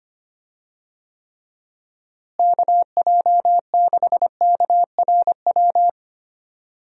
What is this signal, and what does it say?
Morse code, heard as a keyed tone, about 25 wpm, 700 Hz. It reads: KJ6KRW
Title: USA Call Signs - 6 Characters - Repeated 3x - 25wpm